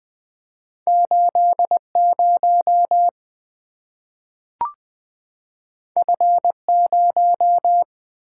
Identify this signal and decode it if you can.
Morse code, heard as a keyed tone, about 20 wpm, 700 Hz. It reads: 80 F0
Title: Two Letter-Number Combinations - 20wpm